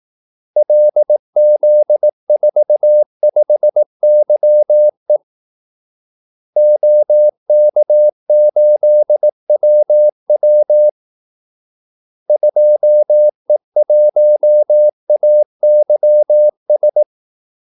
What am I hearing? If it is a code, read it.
Morse code, heard as a keyed tone, about 18 wpm, 600 Hz. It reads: LZ45YE OK8WW 2E1AYS
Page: Kaldesignaler 18 wpm